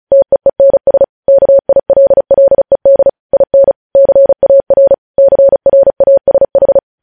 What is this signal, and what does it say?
Morse code, heard as a keyed tone, about 36 wpm, 550 Hz. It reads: TEENS KILLED IN CAR CRASH
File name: podcast / News headlines-1x-35x35